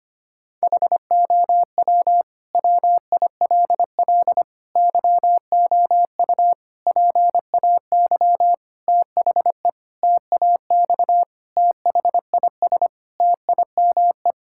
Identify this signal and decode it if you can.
Morse code, heard as a keyed tone, about 25 wpm, 700 Hz. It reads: HOW WILL YOU PAY THE TAX THIS TIME
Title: Sentences from Top 1000 Words - Review Part 5 - 25wpm